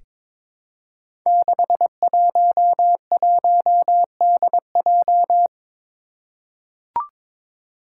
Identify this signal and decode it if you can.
Morse code, heard as a keyed tone, about 22 wpm, 700 Hz. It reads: 611DJ
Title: Five Challenging Letter-Number Combinations 22wpm